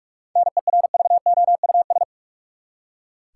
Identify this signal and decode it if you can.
Morse code, heard as a keyed tone, about 48 wpm, 700 Hz. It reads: NERVOUS
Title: Seven Letter Words - 45wpm